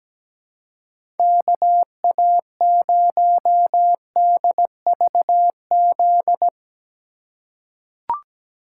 Morse code, heard as a keyed tone, about 17 wpm, 700 Hz. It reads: KA0DVZ
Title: USA Call Signs - 6 Characters - 3x Character Spacing - 17wpm